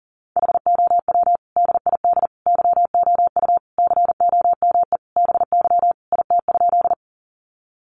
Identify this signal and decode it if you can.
Morse code, heard as a keyed tone, about 40 wpm, 700 Hz. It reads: HOW DID YOU COME BY IT?
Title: Sentences from Top 100 Words 40wpm